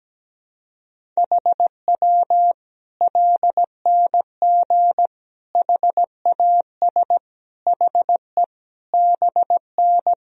Morse code, heard as a keyed tone, about 17 wpm, 700 Hz. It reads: HW LNG HAS HE BN
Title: Sentences from Top 100 Words and Abbreviations 17wpm